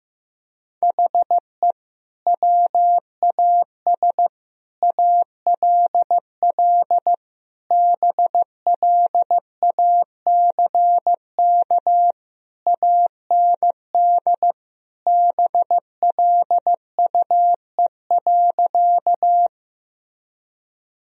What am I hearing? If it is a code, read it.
Morse code, heard as a keyed tone, about 15 wpm, 700 Hz. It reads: HE WAS ALL BLACK AND BLUE.
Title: Sentences from Top 1000 Words - Review Part 3 - 15wpm